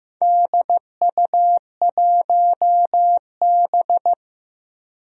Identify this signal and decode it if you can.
Morse code, heard as a keyed tone, about 15 wpm, 700 Hz. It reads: DU1B
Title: DXCC-100 Call Signs - 4 Characters - Repeated 3x - 15wpm